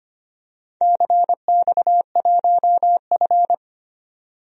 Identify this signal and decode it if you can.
Morse code, heard as a keyed tone, about 25 wpm, 700 Hz. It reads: CX1F
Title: DXCC-100 Call Signs - 4 Characters - 25wpm